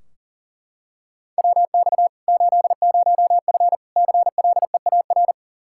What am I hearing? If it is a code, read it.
Morse code, heard as a keyed tone, about 40 wpm, 700 Hz. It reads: WX 80F CLEAR
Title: Practice QSO - Exchange 2 - Info - 40wpm